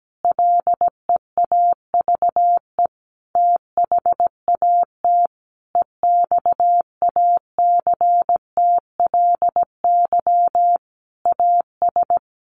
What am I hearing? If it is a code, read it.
Morse code, heard as a keyed tone, about 17 wpm, 700 Hz. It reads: LEAVE THAT EXACTLY AS
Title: Sentences from Top 900 Words 17wpm